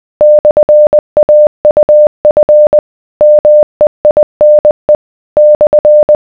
Morse code, heard as a keyed tone, about 20 wpm, 600 Hz. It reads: / AUF MEINE /